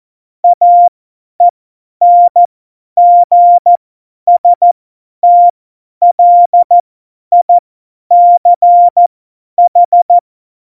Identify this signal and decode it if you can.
Morse code, heard as a keyed tone, about 14 wpm, spaced as for 10 wpm, 700 Hz. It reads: AENGSTLICH
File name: Morsekurs / Witze / Witz 03.Juli 2018, 50 F70